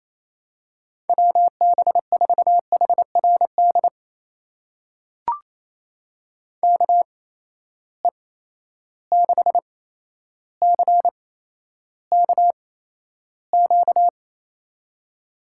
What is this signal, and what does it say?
Morse code, heard as a keyed tone, about 27 wpm, spaced as for 12 wpm, 700 Hz. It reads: WB4HRD KE6CKQ
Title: USA Call Signs - 6 Characters - 8x Character Spacing - 28wpm